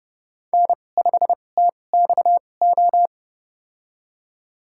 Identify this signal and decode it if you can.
Morse code, heard as a keyed tone, about 32 wpm, spaced as for 22 wpm, 700 Hz. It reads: N5TXO
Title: USA Call Signs - 5 Characters - 2x Character Spacing - 30wpm